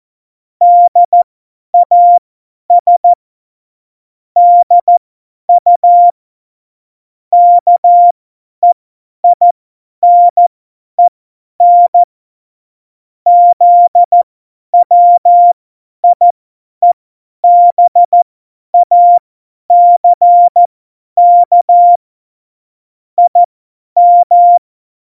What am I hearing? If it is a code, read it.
Morse code, heard as a keyed tone, about 14 wpm, spaced as for 10 wpm, 700 Hz. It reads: DAS DU KEINEN ZWIEBACK IM
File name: Morsekurs / Witze / Witz 18. September 2018, 50 F70